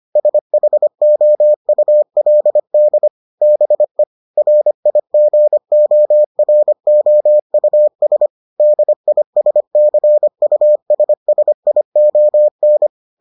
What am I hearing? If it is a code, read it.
Morse code, heard as a keyed tone, about 25 wpm, 600 Hz. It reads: SHOULD BE RIGOROUS DISCUSSION